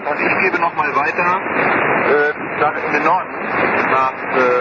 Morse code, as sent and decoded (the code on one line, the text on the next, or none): none